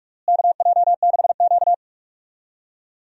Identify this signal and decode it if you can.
Morse code, heard as a keyed tone, about 46 wpm, 700 Hz. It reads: KJ6Q
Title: USA Call Signs - 4 Characters - Repeated 3x - 45wpm